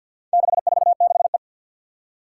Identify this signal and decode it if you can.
Morse code, heard as a keyed tone, about 50 wpm, 700 Hz. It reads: 646E
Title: Four Challenging Letter-Number Combinations 50wpm